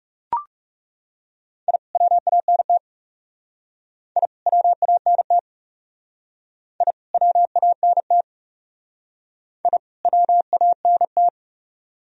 Morse code, heard as a keyed tone, about 37 wpm, 700 Hz. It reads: I WANT I WANT I WANT I WANT
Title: Speed-Racing - Sets of 2 Words 30wpm